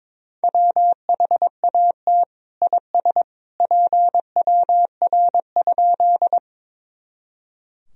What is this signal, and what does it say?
Morse code, heard as a keyed tone, about 22 wpm, 700 Hz. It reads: WHAT IS PWR?
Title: Sentences from Top 500 Words and Abbreviations 22wpm